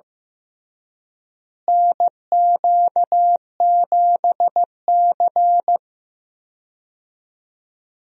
Morse code, heard as a keyed tone, about 15 wpm, 700 Hz. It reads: NQ7C
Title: USA Call Signs - 4 Characters - Repeated 3x - 15wpm